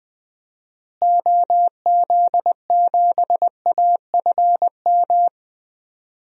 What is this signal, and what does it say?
Morse code, heard as a keyed tone, about 20 wpm, 700 Hz. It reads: OZ7AFM
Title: DXCC-100 Call Signs - 6 Characters - Repeated 3x - 20wpm